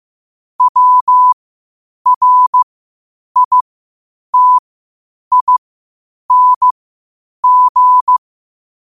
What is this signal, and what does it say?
Morse code, heard as a keyed tone, about 15 wpm, spaced as for 8 wpm, 1000 Hz. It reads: WRITING
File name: qotdfiles / 8wpm